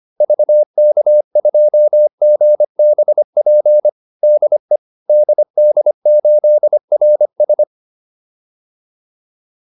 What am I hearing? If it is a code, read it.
Morse code, heard as a keyed tone, about 25 wpm, 600 Hz. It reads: VK2GBP DE DD8RS
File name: QSO405a 25